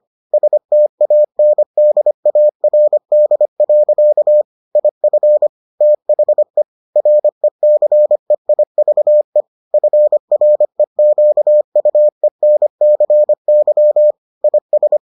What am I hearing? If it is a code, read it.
Morse code, heard as a keyed tone, about 25 wpm, 600 Hz. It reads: STANDARD. IF THE RECEIVE FREQUENCY IS